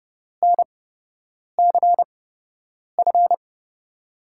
Morse code, heard as a keyed tone, about 30 wpm, spaced as for 8 wpm, 700 Hz. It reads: NCF
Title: USA Call Signs - 5 Characters - 8x Character Spacing - 30wpm